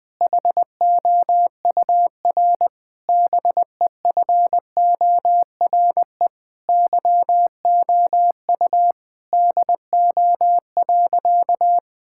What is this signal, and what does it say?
Morse code, heard as a keyed tone, about 20 wpm, 700 Hz. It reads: HOUR BEFORE YOU DO.
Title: Sentences from Top 1000 Words - Review Part 6 - 20wpm